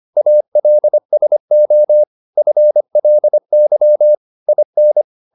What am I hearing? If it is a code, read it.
Morse code, heard as a keyed tone, about 25 wpm, 600 Hz. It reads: ALSO FLY IN